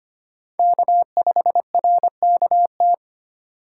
Morse code, heard as a keyed tone, about 32 wpm, 700 Hz. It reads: K5RKT